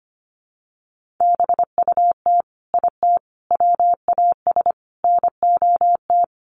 Morse code, heard as a keyed tone, about 25 wpm, 700 Hz. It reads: BUT IT WAS NOT
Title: Sets of 4 Words 25wpm